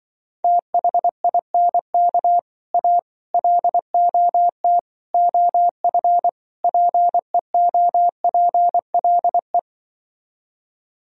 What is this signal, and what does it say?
Morse code, heard as a keyed tone, about 24 wpm, 700 Hz. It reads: THINK A LOT OF PEOPLE
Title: Speed-Racing - Sets of 5 Words 20wpm